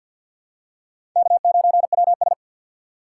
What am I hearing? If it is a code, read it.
Morse code, heard as a keyed tone, about 47 wpm, 700 Hz. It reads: K9WS
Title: USA Call Signs - 4 Characters - 3x Character Spacing - 50wpm